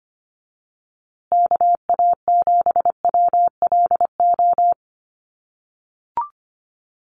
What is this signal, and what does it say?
Morse code, heard as a keyed tone, about 25 wpm, 700 Hz. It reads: KA7WLO
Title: USA Call Signs Repeated 2x 25wpm